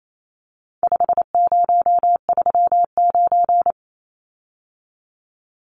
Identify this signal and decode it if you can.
Morse code, heard as a keyed tone, about 28 wpm, 700 Hz. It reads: 5039
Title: Numbers - 4-Digit 28wpm